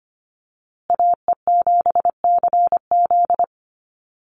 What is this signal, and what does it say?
Morse code, heard as a keyed tone, about 25 wpm, 700 Hz. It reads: AE7CZ
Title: USA Call Signs Repeated 3x 25wpm